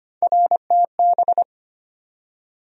Morse code, heard as a keyed tone, about 25 wpm, 700 Hz. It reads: RTB
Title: USA Call Sign Postfixes - 3 Letters 25wpm